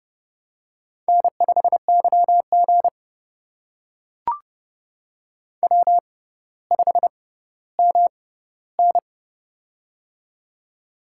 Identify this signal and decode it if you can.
Morse code, heard as a keyed tone, about 30 wpm, spaced as for 15 wpm, 700 Hz. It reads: N5YG W5MN
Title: USA Call Signs - 4 Characters - 6x Character Spacing - 30wpm